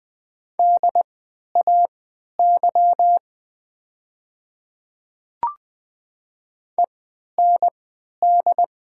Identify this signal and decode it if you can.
Morse code, heard as a keyed tone, about 20 wpm, spaced as for 11 wpm, 700 Hz. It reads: DAY END
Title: 3x Character Spacing - Top 300 Words - 20wpm